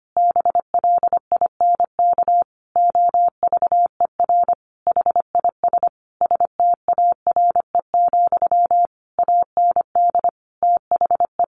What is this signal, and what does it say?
Morse code, heard as a keyed tone, about 25 wpm, 700 Hz. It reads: BLINK OVER HIS STARE, AND THE
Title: A Sound Of Thunder 25wpm